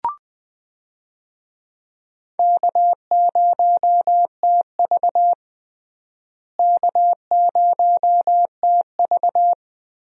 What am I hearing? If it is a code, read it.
Morse code, heard as a keyed tone, about 20 wpm, 700 Hz. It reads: K0TV K0TV
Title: USA Call Signs - 4 Characters - Repeated 3x - 20wpm